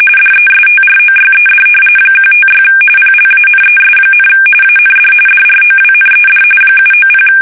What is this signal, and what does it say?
RTTY, about 45.45 baud. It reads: DESTROYED 156 PLANES, AND DROWNED 790
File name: fsk850 / battleshipnj